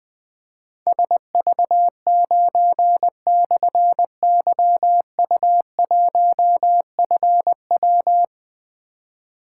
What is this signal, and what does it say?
Morse code, heard as a keyed tone, about 20 wpm, 700 Hz. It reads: SV9/YU1FW
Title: Difficult Call Signs 20wpm